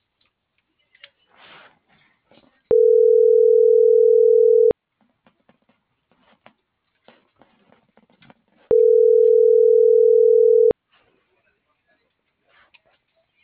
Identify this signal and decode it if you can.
Ringback tone (North America), 440 + 480 Hz.
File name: CENTRAL GOMSA / out garcia - .7607